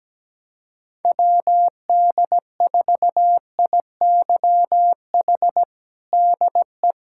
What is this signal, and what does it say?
Morse code, heard as a keyed tone, about 17 wpm, 700 Hz. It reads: WD4IYH DE
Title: Practice QSO - Exchange 4 - Ending - 17wpm